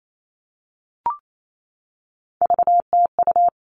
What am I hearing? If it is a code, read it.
Morse code, heard as a keyed tone, about 28 wpm, 700 Hz. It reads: VTU
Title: All Cut Numbers - 3-Digit 28wpm